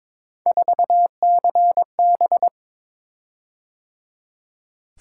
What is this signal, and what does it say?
Morse code, heard as a keyed tone, about 22 wpm, 700 Hz. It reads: 4CB
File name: practice / 22wpm / Three Letter-Number Combinations - 22wpm